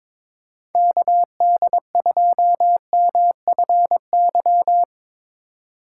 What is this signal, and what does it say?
Morse code, heard as a keyed tone, about 22 wpm, 700 Hz. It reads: KD2MFY
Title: USA Call Signs - 6 Characters - 2x Character Spacing - 22wpm